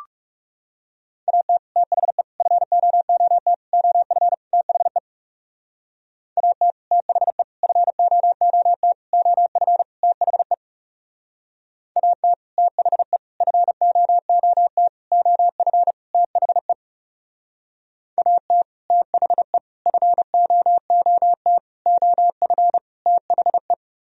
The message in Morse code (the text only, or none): AT THE FOOT OF THE AT THE FOOT OF THE AT THE FOOT OF THE AT THE FOOT OF THE